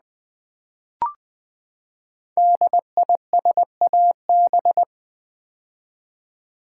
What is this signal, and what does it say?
Morse code, heard as a keyed tone, about 20 wpm, 700 Hz. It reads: DISAB
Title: Build A Word - 5 Letters At A Time - 10-Letter Words - 20wpm